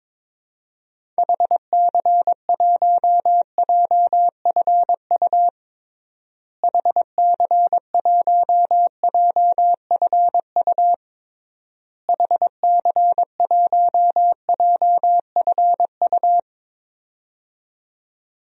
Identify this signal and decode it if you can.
Morse code, heard as a keyed tone, about 22 wpm, 700 Hz. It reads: HC1JFU HC1JFU HC1JFU
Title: DXCC-100 Call Signs - 6 Characters - Repeated 3x - 22wpm